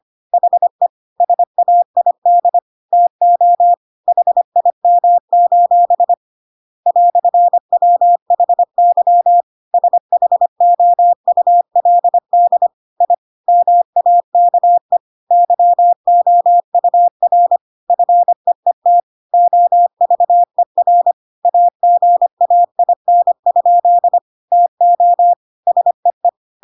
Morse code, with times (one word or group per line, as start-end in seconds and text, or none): 0.3-0.9 HE
1.2-2.6 SAID
2.9-3.7 TO
4.1-6.1 HIM:
6.9-9.4 "WHY
9.7-12.7 SHOULD
13.0-13.1 I
13.5-15.0 MAKE
15.3-17.6 YOUR
17.9-19.0 FEET
19.3-21.1 OVER
21.4-24.2 AGAIN?
24.5-25.3 TO
25.7-26.3 SEE